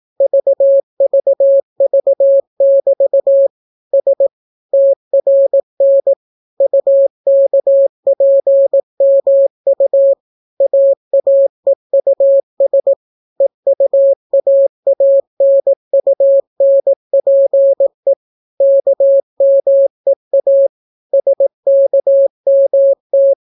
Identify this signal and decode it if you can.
Morse code, heard as a keyed tone, about 18 wpm, 550 Hz. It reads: VVV<BT> S TRN UKPMU AAEUS EUAANUNPE KMEA SKMT